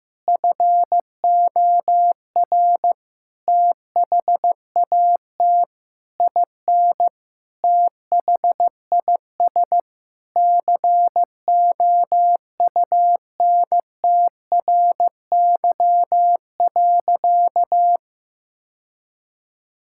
Morse code, heard as a keyed tone, about 16 wpm, 700 Hz. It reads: FOR THAT IN THIS COUNTRY.
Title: Sentences from Top 1000 Words - Review Part 5 - 15wpm